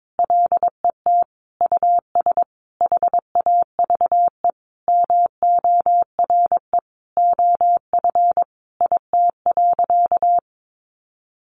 Morse code, heard as a keyed tone, about 22 wpm, 700 Hz. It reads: LET US HAVE MORE OF IT.
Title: Sentences from Top 1300 Words 22wpm